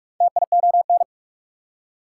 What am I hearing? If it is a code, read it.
Morse code, heard as a keyed tone, about 59 wpm, spaced as for 52 wpm, 690 Hz. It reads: TION
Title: Speed-Racing Prefixes and Suffixes 45wpm